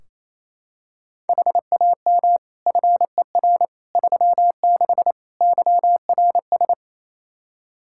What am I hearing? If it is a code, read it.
Morse code, heard as a keyed tone, about 28 wpm, 700 Hz. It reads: HAM FER 36 YRS
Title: Practice QSO - Exchange 3 - More Info - 28wpm